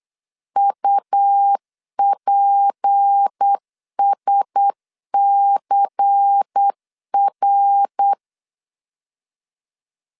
Morse code, heard as a keyed tone, about 8 wpm, 800 Hz. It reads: UPSCR
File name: LEZIONI / 6 PKX0 / 6 P 8wpm 3025